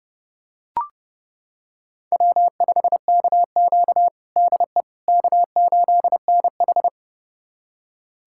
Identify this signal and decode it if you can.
Morse code, heard as a keyed tone, about 30 wpm, 700 Hz. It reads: W5KQ DE K8NH
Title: USA Callsign DE Callsign - 4 Characters - 30wpm